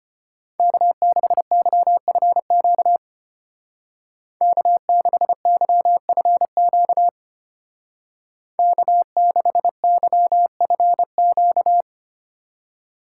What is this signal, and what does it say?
Morse code, heard as a keyed tone, about 33 wpm, 700 Hz. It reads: K6YFQ K6YFQ K6YFQ